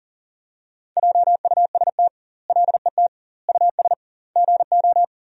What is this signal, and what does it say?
Morse code, heard as a keyed tone, about 40 wpm, 700 Hz. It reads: JUST LET US GO
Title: 2x Word-Spacing - Sentences from Top 500 Words - 40wpm